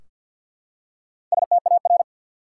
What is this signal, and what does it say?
Morse code, heard as a keyed tone, about 53 wpm, 700 Hz. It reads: STAR
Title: Four Letter Words - 50wpm